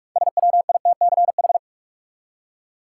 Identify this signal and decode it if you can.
Morse code, heard as a keyed tone, about 46 wpm, 700 Hz. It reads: SWITCH
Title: Six Letter Words - 45wpm